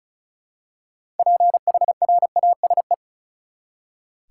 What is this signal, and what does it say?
Morse code, heard as a keyed tone, about 34 wpm, 700 Hz. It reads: PHRASE
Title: Six Letter Words - 35wpm